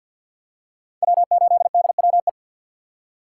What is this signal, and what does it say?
Morse code, heard as a keyed tone, about 52 wpm, 700 Hz. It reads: W8DWE
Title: USA Call Signs - 2x Character Spacing - 50wpm